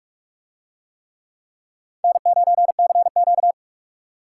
Morse code, heard as a keyed tone, about 44 wpm, 700 Hz. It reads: N9CQ